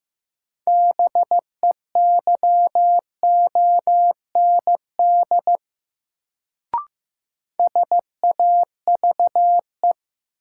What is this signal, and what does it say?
Morse code, heard as a keyed tone, about 16 wpm, 700 Hz. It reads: BEYOND SAVE
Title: Sentences from Top 1100 Words - New Word Drill Alt - 15wpm